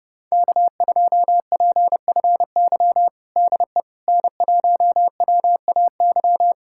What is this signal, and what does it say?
Morse code, heard as a keyed tone, about 30 wpm, 700 Hz. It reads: K2PFY DE N1WAY
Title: USA Callsign DE Callsign 30wpm